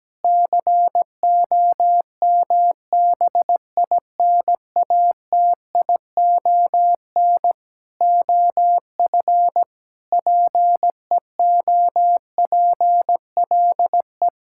Morse code, heard as a keyed tone, about 17 wpm, 700 Hz. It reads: COMBINATION OF PEOPLE
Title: Sentences from Top 1000 Words - Review Part 6 - 17wpm